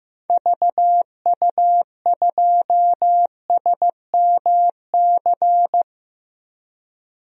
Morse code, heard as a keyed tone, about 15 wpm, 700 Hz. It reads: VU2SMC
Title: DXCC-100 Call Signs - 6 Characters - 15wpm